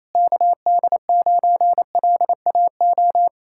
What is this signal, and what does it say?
Morse code, heard as a keyed tone, about 34 wpm, 700 Hz. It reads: KD9LAO
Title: Speed-Racing Common QSO Phrases Encore - 28wpm